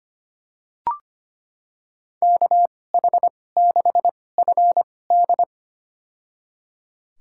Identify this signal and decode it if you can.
Morse code, heard as a keyed tone, about 25 wpm, spaced as for 18 wpm, 700 Hz. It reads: KH6FD
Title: USA Call Signs - 5 Characters - 2x Character Spacing - 25wpm